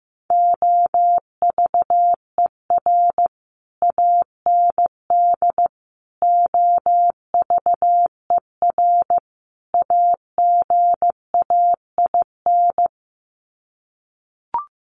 Morse code, heard as a keyed tone, about 15 wpm, 700 Hz. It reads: OVER AND OVER AGAIN
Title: Sets of 4 Words 15wpm